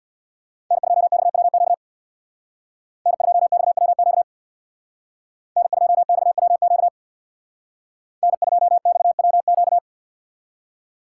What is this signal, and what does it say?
Morse code, heard as a keyed tone, about 61 wpm, 700 Hz. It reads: N2XWQ N2XWQ N2XWQ N2XWQ
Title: Speed Racing - USA 1x3 Call Signs 50wpm